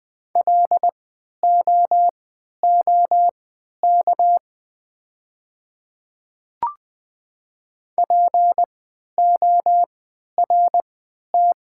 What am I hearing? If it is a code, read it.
Morse code, heard as a keyed tone, about 20 wpm, spaced as for 11 wpm, 690 Hz. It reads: LOOK PORT
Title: 3x Character Spacing - Top 200 Words - 20wpm